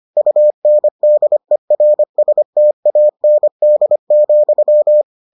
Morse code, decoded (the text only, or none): UNDERSTAND,